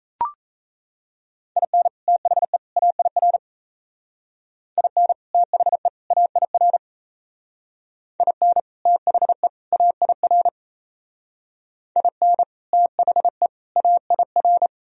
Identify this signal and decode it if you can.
Morse code, heard as a keyed tone, about 33 wpm, 700 Hz. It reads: IN THE AIR IN THE AIR IN THE AIR IN THE AIR
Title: Speed-Racing - Sets of 3 Words 28wpm